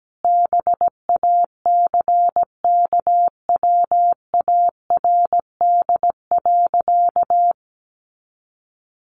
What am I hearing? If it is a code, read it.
Morse code, heard as a keyed tone, about 17 wpm, 700 Hz. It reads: BACKWARD.
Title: A Sound Of Thunder 17wpm